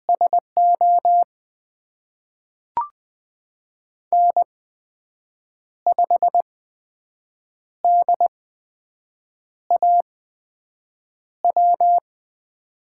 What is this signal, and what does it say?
Morse code, heard as a keyed tone, about 20 wpm, spaced as for 6 wpm, 700 Hz. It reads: SO N5DAW
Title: USA Call Signs - 5 Characters - 8x Character Spacing - 20wpm